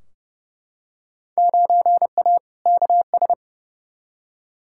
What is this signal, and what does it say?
Morse code, heard as a keyed tone, about 30 wpm, 700 Hz. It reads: 9A KS
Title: Field Day Exchanges Encore - 30wpm